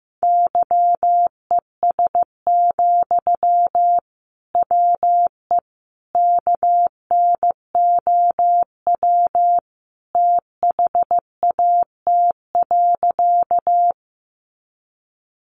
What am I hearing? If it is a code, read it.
Morse code, heard as a keyed tone, about 15 wpm, 700 Hz. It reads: YES, WE KNOW THAT.
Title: Sentences from Top 500 Words 15wpm